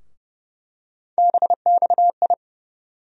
Morse code, heard as a keyed tone, about 29 wpm, 700 Hz. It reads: BXI
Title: USA Call Sign Postfixes - 3 Letters 30wpm